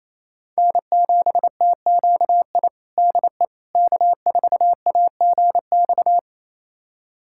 Morse code, heard as a keyed tone, about 28 wpm, 700 Hz. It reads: N7TQI DE K4AGX
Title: USA Callsign DE Callsign 28wpm